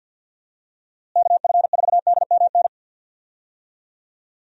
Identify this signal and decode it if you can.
Morse code, heard as a keyed tone, about 51 wpm, 700 Hz. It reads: KF4DMN
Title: USA Call Signs - 6 Characters - Repeated 3x - 50wpm